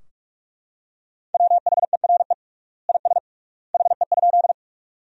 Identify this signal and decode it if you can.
Morse code, heard as a keyed tone, about 42 wpm, 700 Hz. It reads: WHERE IS HE?
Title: 3x Word-Spacing - Sentences from Top 200 Words - 45wpm